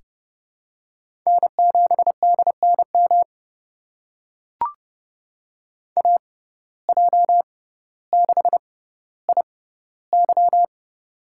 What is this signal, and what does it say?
Morse code, heard as a keyed tone, about 30 wpm, spaced as for 15 wpm, 700 Hz. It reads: N7DNM AJ6IY
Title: USA Call Signs - 5 Characters - 6x Character Spacing - 30wpm